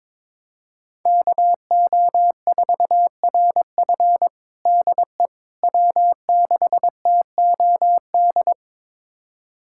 Morse code, heard as a keyed tone, about 22 wpm, 700 Hz. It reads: KO4RF DE W6TOD
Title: USA Callsign DE Callsign - 5 Characters - 22wpm